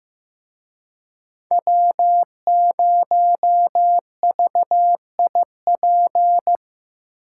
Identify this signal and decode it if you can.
Morse code, heard as a keyed tone, about 18 wpm, 700 Hz. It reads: W0VIP
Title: Speed Racing - USA 1x3 Call Signs 15wpm